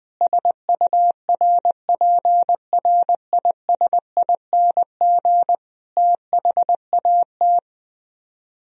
Morse code, heard as a keyed tone, about 20 wpm, 700 Hz. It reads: SURPRISING THAT
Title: Speed-Racing - Sets of 5 Words 17wpm